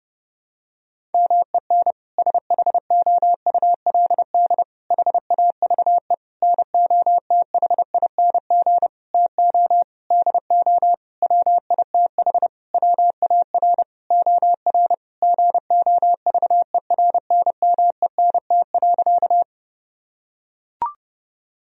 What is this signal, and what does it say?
Morse code, heard as a keyed tone, about 30 wpm, 700 Hz. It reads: MEN SHOULD HAVE NOTHING TO DO WITH WAR OR GOVERNMENT.
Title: Sentences from Top 1000 Words - Review Part 3 - 30wpm